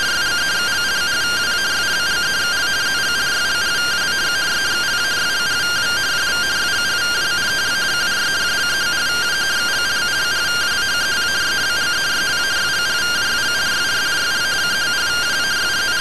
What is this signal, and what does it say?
Two-tone teleprinter signal, about 170 Hz shift.